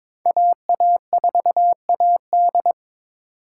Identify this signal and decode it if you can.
Morse code, heard as a keyed tone, about 26 wpm, 700 Hz. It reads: AA4AD
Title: Speed Racing - USA 2x2 Call Signs 22wpm